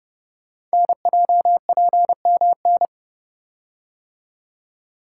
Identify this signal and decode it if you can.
Morse code, heard as a keyed tone, about 28 wpm, 700 Hz. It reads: NJPMN
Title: Code Groups - 5 Letters - 30wpm